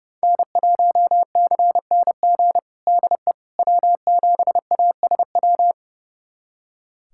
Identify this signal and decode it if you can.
Morse code, heard as a keyed tone, about 32 wpm, 700 Hz. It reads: N1CNG DE W7ASW
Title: USA Callsign DE Callsign - 5 Characters - 30wpm